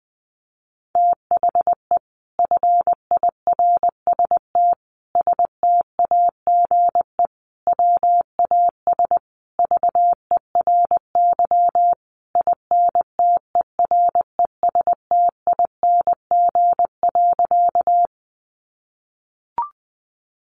Morse code, heard as a keyed tone, about 20 wpm, 700 Hz. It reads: THE FIRST STAGE WAS VERY INTERESTING.
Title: Sentences from Top 1200 Words 20wpm